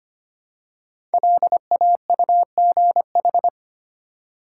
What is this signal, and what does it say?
Morse code, heard as a keyed tone, about 25 wpm, 700 Hz. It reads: LAUGH
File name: 5 Letter Words - First 3 Letters Given - 25wpm